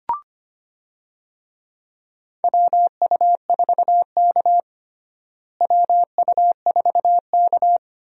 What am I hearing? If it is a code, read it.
Morse code, heard as a keyed tone, about 25 wpm, 700 Hz. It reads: WU4K WU4K
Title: USA Call Signs - 4 Characters - Repeated 3x - 25wpm